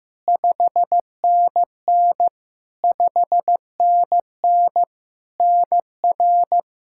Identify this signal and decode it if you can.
Morse code, heard as a keyed tone, about 15 wpm, 700 Hz. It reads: 5NN 5NN NR
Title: Practice QSO - Exchange 1 - Beginning - 15wpm